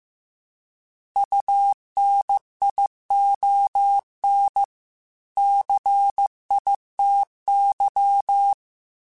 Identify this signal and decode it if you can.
Morse code, heard as a keyed tone, about 15 wpm, 800 Hz. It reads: UNION CITY